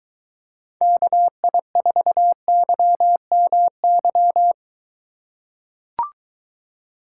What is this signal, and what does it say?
Morse code, heard as a keyed tone, about 23 wpm, 700 Hz. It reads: KI4YMY